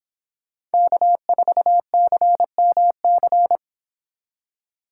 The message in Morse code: K4CMC

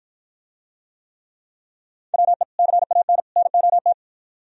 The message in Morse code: WE CAN NOT

